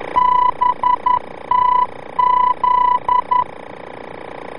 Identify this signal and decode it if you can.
Morse code, heard as a keyed tone, about 11 wpm, 990 Hz. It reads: BTZ
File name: BTZ-VOR-F